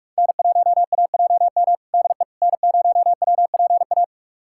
Morse code, heard as a keyed tone, about 44 wpm, 700 Hz. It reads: N1AJM DE N0WPA
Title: USA Callsign DE Callsign 45wpm